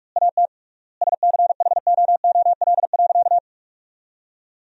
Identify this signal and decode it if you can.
Morse code, heard as a keyed tone, about 46 wpm, 700 Hz. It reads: AT SCHOOL.
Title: 3x Word-Spacing - Sentences from Top 300 Words - 45wpm